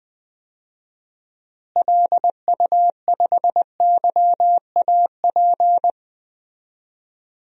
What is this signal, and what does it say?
Morse code, heard as a keyed tone, about 20 wpm, 700 Hz. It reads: LU5YAP
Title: DXCC-100 Call Signs - 6 Characters - Repeated 3x - 20wpm